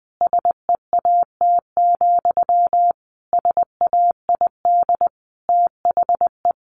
Morse code, heard as a keyed tone, about 20 wpm, 700 Hz. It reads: SEAT, SAID THE
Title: Sentences from Top 1500 Words 20wpm